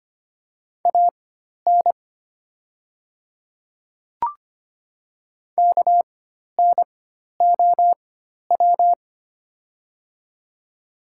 Morse code, heard as a keyed tone, about 25 wpm, spaced as for 12 wpm, 700 Hz. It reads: AN KNOW
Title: 4x Character Spacing - Top 100 Words - 25wpm